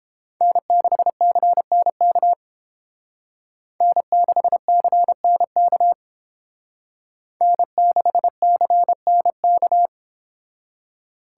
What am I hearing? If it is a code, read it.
Morse code, heard as a keyed tone, about 31 wpm, 700 Hz. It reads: N6CNK N6CNK N6CNK